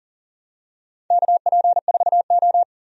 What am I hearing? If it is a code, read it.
Morse code, heard as a keyed tone, about 40 wpm, 700 Hz. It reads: KP4O
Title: DXCC-100 Call Signs - 4 Characters - 40wpm